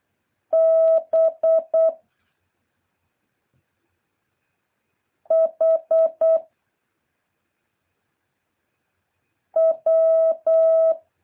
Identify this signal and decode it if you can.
Morse code, heard as a keyed tone, about 8 wpm, 650 Hz. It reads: B H W
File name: jh8cbh / 2023cw